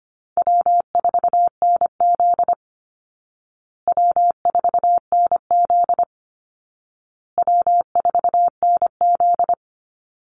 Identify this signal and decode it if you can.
Morse code, heard as a keyed tone, about 25 wpm, 700 Hz. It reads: W4NZ W4NZ W4NZ
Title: Call Signs Repeated 2x - 25wpm